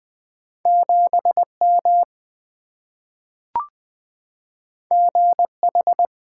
Morse code, heard as a keyed tone, about 20 wpm, 700 Hz. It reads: 7M GH
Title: Two Letter-Number Combinations - 20wpm